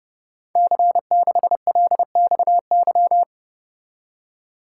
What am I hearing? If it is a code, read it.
Morse code, heard as a keyed tone, about 30 wpm, 700 Hz. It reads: C6LXY
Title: Code Groups - 5 Letter:Numbers - 30wpm